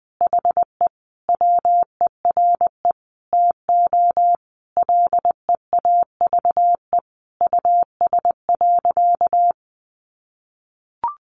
Morse code, heard as a keyed tone, about 20 wpm, 700 Hz. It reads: HE WERE TO LEAVE US.
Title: Sentences from Top 400 Words 20wpm